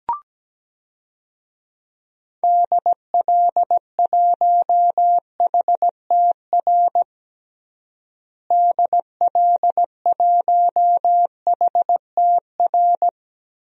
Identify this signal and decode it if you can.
Morse code, heard as a keyed tone, about 17 wpm, 700 Hz. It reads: DL1HTR DL1HTR
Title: DXCC-100 Call Signs - 6 Characters - Repeated 2x - 17wpm